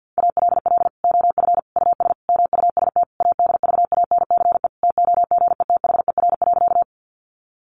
Morse code, heard as a keyed tone, about 50 wpm, 700 Hz. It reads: ALL OF US MUST ADVANCE TOGETHER.
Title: Sentences from Top 1400 Words 50wpm